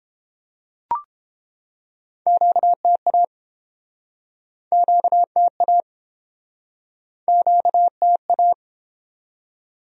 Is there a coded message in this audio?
Yes: Morse code from 2.3 to 8.5 s, heard as a keyed tone, about 30 wpm, 700 Hz. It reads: QTA QTA QTA